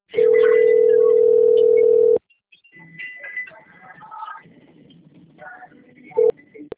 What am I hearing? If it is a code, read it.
Ringback tone (North America), 440 + 480 Hz.
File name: out- - - - - .67854